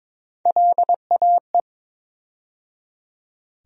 Morse code, heard as a keyed tone, about 22 wpm, 700 Hz. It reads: LAE
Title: USA Call Sign Postfixes - 3 Letters 22wpm